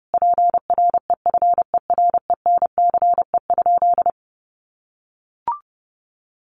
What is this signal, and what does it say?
Morse code, heard as a keyed tone, about 30 wpm, 700 Hz. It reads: PREFERENCE?
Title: Sentences from Top 1500 Words 30wpm